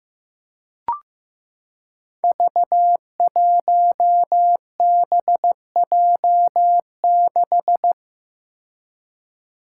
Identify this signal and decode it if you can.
Morse code, heard as a keyed tone, about 15 wpm, 700 Hz. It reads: V1BJ6
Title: Five Challenging Letter-Number Combinations 15wpm